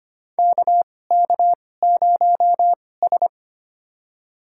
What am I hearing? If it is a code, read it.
Morse code, heard as a keyed tone, about 25 wpm, spaced as for 18 wpm, 700 Hz. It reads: KK0S
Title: USA Call Signs - 4 Characters - 2x Character Spacing - 25wpm